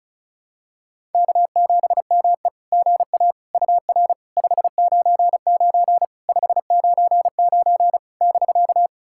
Morse code, heard as a keyed tone, about 36 wpm, 690 Hz. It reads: K7ME GA UR 599 599 <BK>
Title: SOTA QSO Practice 35wpm